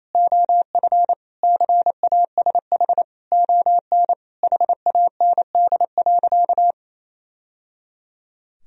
Morse code, heard as a keyed tone, about 28 wpm, 700 Hz. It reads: OF CASH ON HAND.
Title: Sentences from Top 1000 Words - Review Part 4 - 28wpm